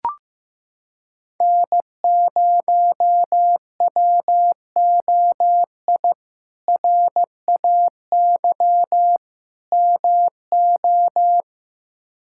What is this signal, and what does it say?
Morse code, heard as a keyed tone, about 15 wpm, 700 Hz. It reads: N0WOI RAY MO
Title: K1USN SST QSO Practice - 15wpm